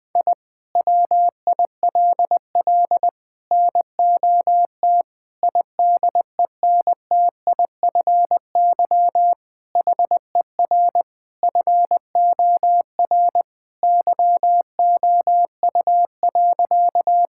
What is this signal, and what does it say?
Morse code, heard as a keyed tone, about 21 wpm, 700 Hz. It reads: I WILL NOT IDENTIFY HER FOR YOU.
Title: Sentences from Top 1000 Words - Review Part 6 - 20wpm